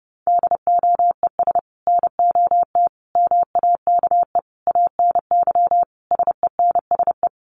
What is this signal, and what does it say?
Morse code, heard as a keyed tone, about 30 wpm, 700 Hz. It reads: DOES NOT MAKE ANY SENSE
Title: Sets of 5 Words Encore2 30wpm